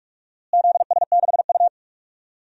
Morse code, heard as a keyed tone, about 46 wpm, 700 Hz. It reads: ZS6U
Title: DXCC-100 Call Signs - 4 Characters - Repeated 2x - 45wpm